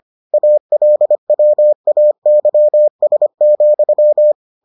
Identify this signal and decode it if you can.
Morse code, heard as a keyed tone, about 25 wpm, 600 Hz. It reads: ALWAYS,